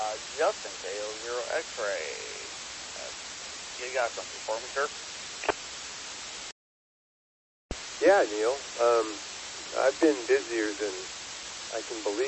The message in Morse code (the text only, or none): none